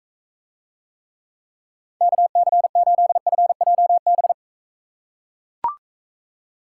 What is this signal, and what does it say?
Morse code, heard as a keyed tone, about 40 wpm, 700 Hz. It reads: KC8FJB